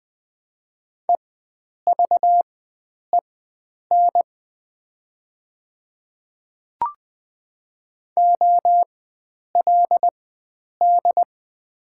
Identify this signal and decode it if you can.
Morse code, heard as a keyed tone, about 20 wpm, spaced as for 9 wpm, 700 Hz. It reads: EVEN OLD
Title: 4x Character Spacing - Top 300 Words - 20wpm